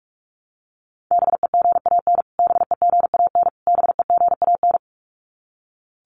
Morse code, heard as a keyed tone, about 44 wpm, 700 Hz. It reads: BEGAN BEGAN BEGAN
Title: Top 300 Words Repeated 45wpm